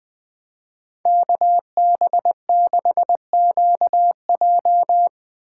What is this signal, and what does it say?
Morse code, heard as a keyed tone, about 20 wpm, 700 Hz. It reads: KB6QJ